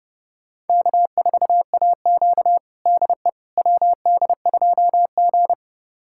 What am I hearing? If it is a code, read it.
Morse code, heard as a keyed tone, about 30 wpm, 700 Hz. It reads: K4AQ DE WD2G
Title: USA Callsign DE Callsign - 4 Characters - 30wpm